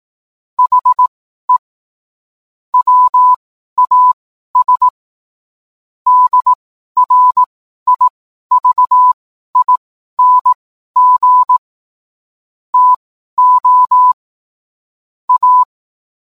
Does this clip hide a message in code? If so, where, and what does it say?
Yes: Morse code from 0.6 to 15.6 s, heard as a keyed tone, about 18 wpm, spaced as for 12 wpm, 1000 Hz. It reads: HE WAS DRIVING TO A